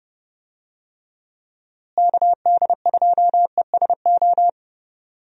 Morse code, heard as a keyed tone, about 30 wpm, 700 Hz. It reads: KD2ESO
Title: USA Call Signs - 6 Characters - Repeated 3x - 30wpm